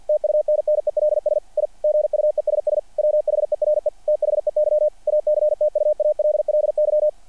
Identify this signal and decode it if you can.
Morse code, heard as a keyed tone, about 48 wpm, 600 Hz. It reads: TUNNELS I GUESS WHERE THEY ACTUALLY
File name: archived / CRSnet